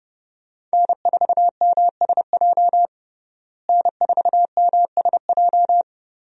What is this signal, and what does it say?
Morse code, heard as a keyed tone, about 30 wpm, 700 Hz. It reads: N4MSJ N4MSJ
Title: USA Call Signs - 5 Characters - Repeated 2x - 30wpm